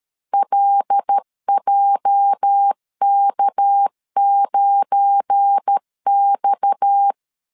Morse code, heard as a keyed tone, about 13 wpm, 800 Hz. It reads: LJK9X
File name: LEZIONI / 8 YJ45 / 8 YJ 12wpm 2235